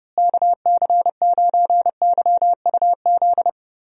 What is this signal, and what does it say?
Morse code, heard as a keyed tone, about 30 wpm, 700 Hz. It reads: KC9YUZ
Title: USA Call Signs - 6 Characters - Repeated 2x - 30wpm